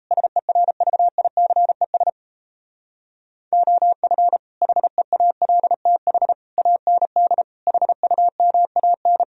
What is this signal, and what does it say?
Morse code, heard as a keyed tone, about 38 wpm, 700 Hz. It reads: SERVICES OF HEALTH AND HUMAN